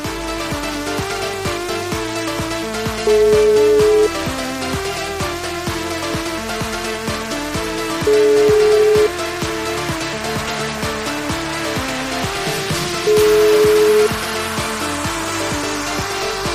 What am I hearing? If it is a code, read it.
Ringback tone (Europe, 425 Hz), with noise.